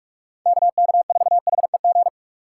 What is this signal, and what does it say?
Morse code, heard as a keyed tone, about 46 wpm, 700 Hz. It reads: KK4HEG
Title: K1USN SST QSO Practice - 45wpm